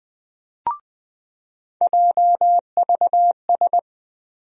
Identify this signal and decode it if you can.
Morse code, heard as a keyed tone, about 20 wpm, 700 Hz. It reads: JVS
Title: USA Call Sign Postfixes - 3 Letters 20wpm